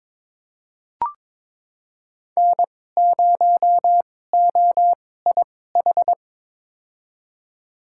Morse code, heard as a keyed tone, about 22 wpm, spaced as for 16 wpm, 700 Hz. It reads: N0OIH
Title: USA Call Signs - 5 Characters - 2x Character Spacing - 22wpm